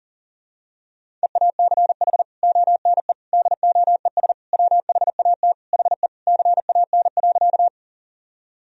Morse code, heard as a keyed tone, about 40 wpm, 700 Hz. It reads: EACH ONE DOES WHAT HE CAN.
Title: Sentences from Top 1000 Words - Review Part 1 - 40wpm